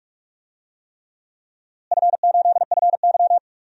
Morse code, heard as a keyed tone, about 46 wpm, 700 Hz. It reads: F8FY
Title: DXCC-100 Call Signs - 4 Characters - Repeated 3x - 45wpm